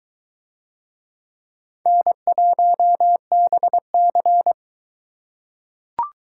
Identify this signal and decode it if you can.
Morse code, heard as a keyed tone, about 22 wpm, 700 Hz. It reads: N1BC